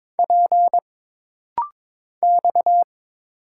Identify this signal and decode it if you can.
Morse code, heard as a keyed tone, about 22 wpm, 700 Hz. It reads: P X
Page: This rapid-fire timing is geared to push your ICR (Instant Character Recognition) skills.